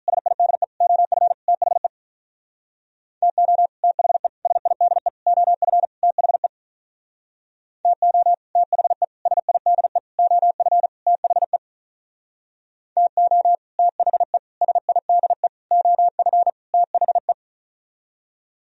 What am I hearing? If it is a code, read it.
Morse code, heard as a keyed tone, about 46 wpm, 700 Hz. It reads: SIDE OF THE TO THE SIDE OF THE TO THE SIDE OF THE TO THE SIDE OF THE